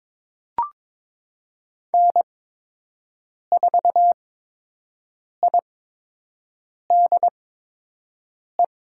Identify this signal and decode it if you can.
Morse code, heard as a keyed tone, about 21 wpm, spaced as for 6 wpm, 700 Hz. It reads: N4IDE
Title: USA Call Signs - 8x Character Spacing - 22wpm